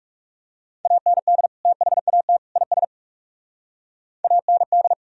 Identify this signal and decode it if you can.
Morse code, heard as a keyed tone, about 40 wpm, 700 Hz. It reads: AND THAT IS AND